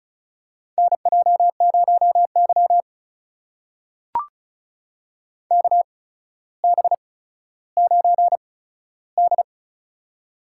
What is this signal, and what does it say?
Morse code, heard as a keyed tone, about 36 wpm, spaced as for 15 wpm, 700 Hz. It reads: NJ0Y KB9D
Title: USA Call Signs - 4 Characters - 8x Character Spacing - 35wpm